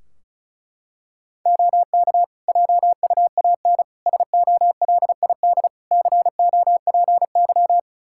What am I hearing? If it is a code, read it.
Morse code, heard as a keyed tone, about 34 wpm, 700 Hz. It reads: OK JUAN SOLID COPY
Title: Practice QSO - Exchange 3 - More Info - 35wpm